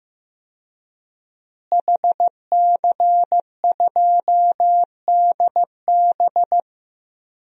Morse code, heard as a keyed tone, about 15 wpm, 700 Hz. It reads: HC2DB
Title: DXCC-100 Call Signs - 5 Characters - Repeated 2x - 15wpm